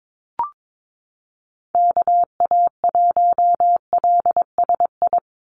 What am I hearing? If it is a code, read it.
Morse code, heard as a keyed tone, about 22 wpm, 700 Hz. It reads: KA1LSI
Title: USA Call Signs 22wpm